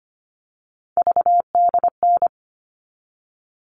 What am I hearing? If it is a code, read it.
Morse code, heard as a keyed tone, about 25 wpm, 700 Hz. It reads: VDN
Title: All Cut Numbers - 3-Digit 25wpm